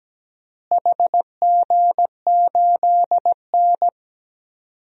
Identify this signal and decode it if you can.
Morse code, heard as a keyed tone, about 17 wpm, 700 Hz. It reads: HG8N
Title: DXCC-100 Call Signs - 4 Characters - Repeated 2x - 17wpm